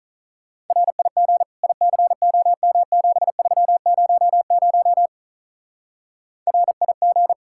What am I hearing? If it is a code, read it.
Morse code, heard as a keyed tone, about 43 wpm, 700 Hz. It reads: RIG ICOM7300 RIG
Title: Speed-Racing Common QSO Phrases Encore - 35wpm